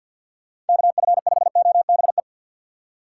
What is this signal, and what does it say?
Morse code, heard as a keyed tone, about 51 wpm, 700 Hz. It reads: KF5OBE
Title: USA Call Signs - 6 Characters - 6x Character Spacing - 50wpm